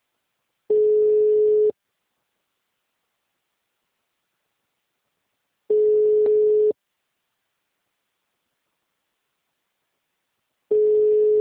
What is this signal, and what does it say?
Ringback tone (Europe, 425 Hz).